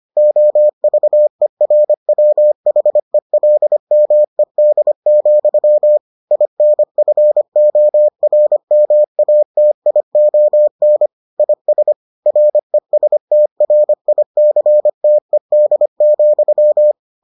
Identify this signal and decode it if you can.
Morse code, heard as a keyed tone, about 25 wpm, 600 Hz. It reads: OVERWHELMED, INFORMATION IS RESTRICTED,